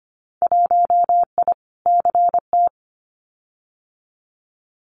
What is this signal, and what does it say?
Morse code, heard as a keyed tone, about 25 wpm, 700 Hz. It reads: 1I CT
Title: Winter Field Day Exchanges 25wpm